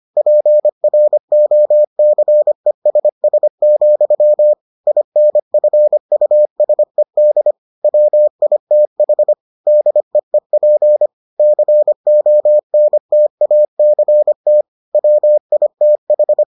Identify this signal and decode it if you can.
Morse code, heard as a keyed tone, about 25 wpm, 600 Hz. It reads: PROCESS, INFUSED WITH DEEP CONTACT WITH